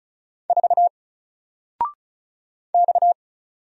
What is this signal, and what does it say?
Morse code, heard as a keyed tone, about 35 wpm, 700 Hz. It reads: 4 X
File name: Single-Letter-Number-35wpm